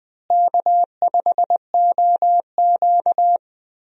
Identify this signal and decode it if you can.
Morse code, heard as a keyed tone, about 21 wpm, 700 Hz. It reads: K5OQ
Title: USA Call Signs - 4 Characters - Repeated 2x - 20wpm